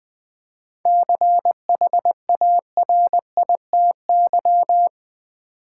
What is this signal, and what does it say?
Morse code, heard as a keyed tone, about 20 wpm, 690 Hz. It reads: CHARITY
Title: 7 Letter Words - First 3 Letters Given - 20wpm